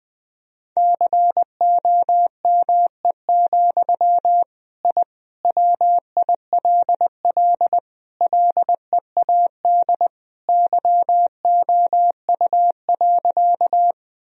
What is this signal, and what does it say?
Morse code, heard as a keyed tone, about 20 wpm, 700 Hz. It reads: COME, I WILL LEAD YOU.
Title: Sentences from Top 1000 Words - Review Part 2 - 20wpm